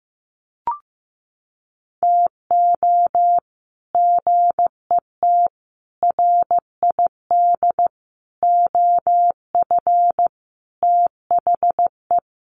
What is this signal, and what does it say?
Morse code, heard as a keyed tone, about 15 wpm, 700 Hz. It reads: TO GET RID OF THE
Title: Sets of 5 Words 15wpm